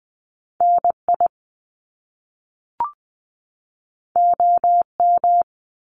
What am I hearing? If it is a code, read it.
Morse code, heard as a keyed tone, about 20 wpm, 700 Hz. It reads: NI OM
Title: Top Two Letter Combinations in Words 20wpm